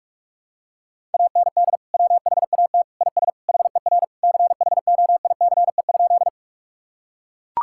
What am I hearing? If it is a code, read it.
Morse code, heard as a keyed tone, about 41 wpm, 700 Hz. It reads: AND WHAT IS HER CHOICE?
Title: Sentences from Top 1000 Words - Review Part 6 - 45wpm